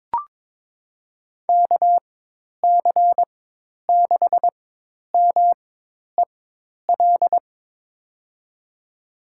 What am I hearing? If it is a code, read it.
Morse code, heard as a keyed tone, about 21 wpm, spaced as for 10 wpm, 700 Hz. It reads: KC6MEL
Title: USA Call Signs - 4x Character Spacing - 22wpm